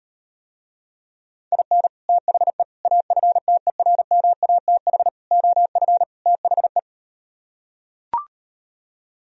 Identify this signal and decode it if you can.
Morse code, heard as a keyed tone, about 38 wpm, 700 Hz. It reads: IN THE AFTERMATH OF THE